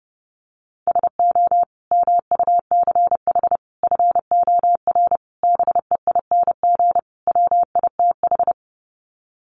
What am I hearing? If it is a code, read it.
Morse code, heard as a keyed tone, about 30 wpm, 700 Hz. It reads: SO MUCH FOR BEING WITH
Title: Sets of 5 Words Encore 30wpm